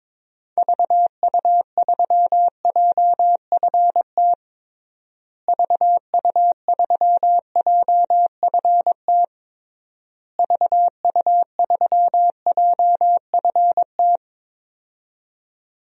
Morse code, heard as a keyed tone, about 22 wpm, 700 Hz. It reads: VU3JFT VU3JFT VU3JFT
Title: DXCC-100 Call Signs - 6 Characters - Repeated 3x - 22wpm